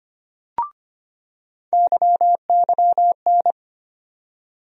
Morse code, heard as a keyed tone, about 25 wpm, 700 Hz. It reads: YYN
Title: USA Call Sign Postfixes - 3 Letters 25wpm